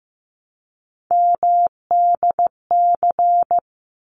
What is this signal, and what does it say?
Morse code, heard as a keyed tone, about 15 wpm, 700 Hz. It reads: MDC
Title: ARRL Sections 15wpm